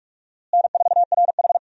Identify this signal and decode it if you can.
Morse code, heard as a keyed tone, about 44 wpm, 700 Hz. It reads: N4RH
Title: USA Call Signs - 4 Characters - Repeated 2x - 45wpm